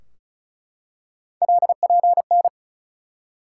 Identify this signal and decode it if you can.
Morse code, heard as a keyed tone, about 36 wpm, 700 Hz. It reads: LPN
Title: Practice QSO - Exchange 3 - More Info - 35wpm